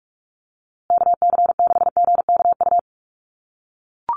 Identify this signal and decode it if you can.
Morse code, heard as a keyed tone, about 44 wpm, 700 Hz. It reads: KC6GKU
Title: USA Call Signs Repeated 3x 45wpm